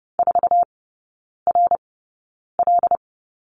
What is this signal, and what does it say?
Morse code, heard as a keyed tone, about 30 wpm, spaced as for 9 wpm, 700 Hz. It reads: 4RL
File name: 081 Lesson 14e - Number 2 - Three Characters 30wpm